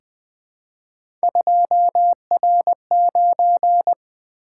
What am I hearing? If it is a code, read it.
Morse code, heard as a keyed tone, about 20 wpm, 700 Hz. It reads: 2R9
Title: Three Letter-Number Combinations - 20wpm